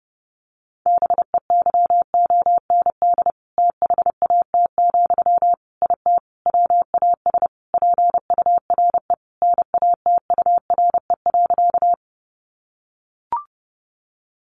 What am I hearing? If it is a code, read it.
Morse code, heard as a keyed tone, about 30 wpm, 700 Hz. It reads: BEYOND THAT, IT WAS PURE NATURE.
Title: Sentences from Top 1500 Words 30wpm